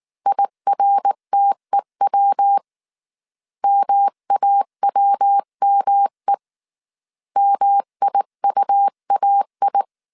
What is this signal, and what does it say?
Morse code, heard as a keyed tone, about 19 wpm, 800 Hz. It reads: IRTEW MAWME MIUAI
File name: CW / DET / 1 3 1500C S B 18wpm 1916